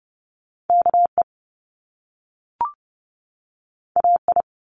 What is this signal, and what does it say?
Morse code, heard as a keyed tone, about 30 wpm, 700 Hz. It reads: KE AI
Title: Top Two Letter Combinations in Words 30wpm